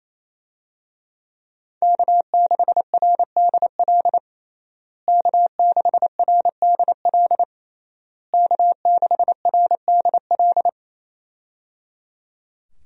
Morse code, heard as a keyed tone, about 28 wpm, 700 Hz. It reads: K6RDL K6RDL K6RDL
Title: USA Call Signs - 5 Characters - Repeated 3x - 28wpm